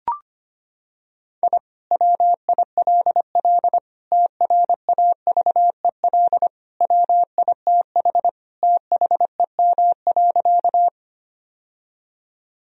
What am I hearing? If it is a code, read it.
Morse code, heard as a keyed tone, about 25 wpm, 700 Hz. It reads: I WILL TRAVEL WITH THEM.
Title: Sentences from Top 1000 Words - Review Part 2 - 25wpm